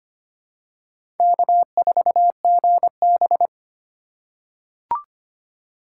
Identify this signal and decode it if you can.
Morse code, heard as a keyed tone, about 25 wpm, 700 Hz. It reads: K4GB
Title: USA Call Signs - 4 Characters - 2x Character Spacing - 25wpm